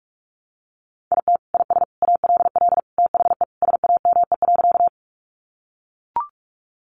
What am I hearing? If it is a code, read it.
Morse code, heard as a keyed tone, about 44 wpm, 700 Hz. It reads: IT IS ALL THE SAME.
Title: Sentences from Top 200 Words 45wpm